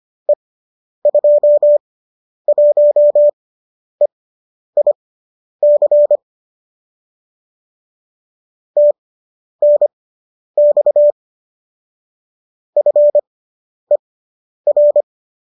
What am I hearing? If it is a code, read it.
Morse code, heard as a keyed tone, about 25 wpm, spaced as for 10 wpm, 590 Hz. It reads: E21EIC TNX FER